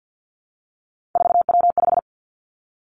Morse code, heard as a keyed tone, about 50 wpm, 700 Hz. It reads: 4W5
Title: Three Challenging Letter-Number Combinations 50wpm